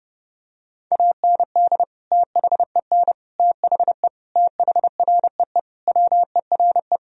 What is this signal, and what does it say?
Morse code, heard as a keyed tone, about 30 wpm, 700 Hz. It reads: AND THEN THE THREE WERE
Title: Sentences from Top 1000 Words - Review Part 1 - 30wpm